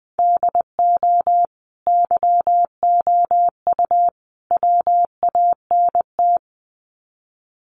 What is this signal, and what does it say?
Morse code, heard as a keyed tone, about 20 wpm, 700 Hz. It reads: DO YOU WANT
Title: Sets of 3 Words 20wpm